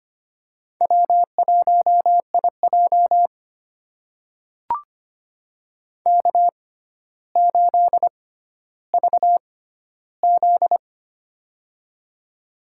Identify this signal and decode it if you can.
Morse code, heard as a keyed tone, about 25 wpm, spaced as for 13 wpm, 700 Hz. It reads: W1IJ K8VZ
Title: USA Call Signs - 4 Characters - 6x Character Spacing - 25wpm